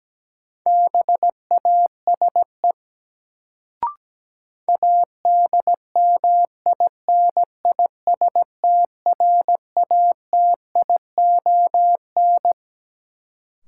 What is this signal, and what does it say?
Morse code, heard as a keyed tone, about 17 wpm, 700 Hz. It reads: BASE ADMINISTRATION
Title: Sets of 5 Words - New Word Drill - 17wpm